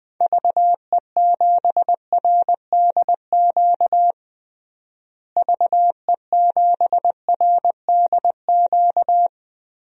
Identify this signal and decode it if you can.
Morse code, heard as a keyed tone, about 20 wpm, 700 Hz. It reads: VE7RDQ VE7RDQ
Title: DXCC-100 Call Signs - 6 Characters - Repeated 3x - 20wpm